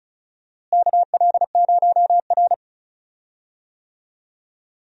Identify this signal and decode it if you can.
Morse code, heard as a keyed tone, about 36 wpm, 700 Hz. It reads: KL0R
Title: USA Call Signs - 4 Characters - Repeated 2x - 35wpm